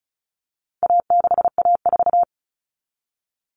Morse code, heard as a keyed tone, about 34 wpm, 700 Hz. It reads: A6A4
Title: All Cut Numbers - 4-Digit 35wpm